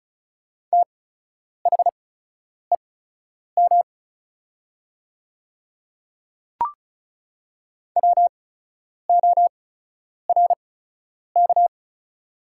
Morse code, heard as a keyed tone, about 36 wpm, spaced as for 10 wpm, 700 Hz. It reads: THEM WORK
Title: 8x Character Spacing - Top 200 Words - 35wpm